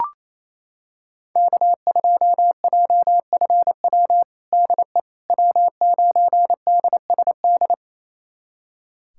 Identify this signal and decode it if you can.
Morse code, heard as a keyed tone, about 27 wpm, 700 Hz. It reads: K2JFW DE W9DSD
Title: USA Callsign DE Callsign - 5 Characters - 28wpm